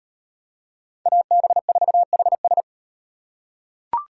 Morse code, heard as a keyed tone, about 35 wpm, 700 Hz. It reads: AB4HS